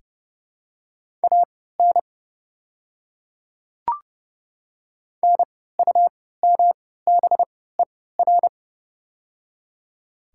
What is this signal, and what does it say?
Morse code, heard as a keyed tone, about 30 wpm, spaced as for 17 wpm, 700 Hz. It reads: AN NUMBER
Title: 3x Character Spacing - Top 100 Words - 30wpm